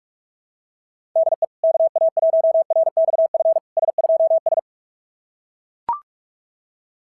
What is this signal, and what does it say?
Morse code, heard as a keyed tone, about 42 wpm, 660 Hz. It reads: DE KA1RXF S2S
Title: SOTA QSO Practice 45wpm